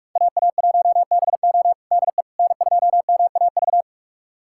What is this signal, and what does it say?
Morse code, heard as a keyed tone, about 46 wpm, 700 Hz. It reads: AA1BO DE N2MAV
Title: USA Callsign DE Callsign - 5 Characters - 45wpm